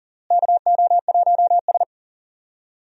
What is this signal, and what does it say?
Morse code, heard as a keyed tone, about 39 wpm, 700 Hz. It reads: KO1S
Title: USA Call Signs - 4 Characters - Repeated 3x - 40wpm